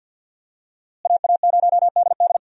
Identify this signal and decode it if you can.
Morse code, heard as a keyed tone, about 50 wpm, 700 Hz. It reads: AA0DD
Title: USA Call Signs - 5 Characters - Repeated 3x - 50wpm